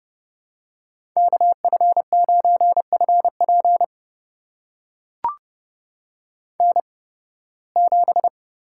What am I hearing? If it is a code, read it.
Morse code, heard as a keyed tone, about 30 wpm, spaced as for 20 wpm, 700 Hz. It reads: KF9FP N7
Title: USA Call Signs - 5 Characters - 8x Character Spacing - 30wpm